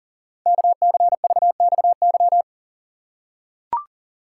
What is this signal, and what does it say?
Morse code, heard as a keyed tone, about 41 wpm, 690 Hz. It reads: KCVXY
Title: Code Groups - 5 Letters - 40wpm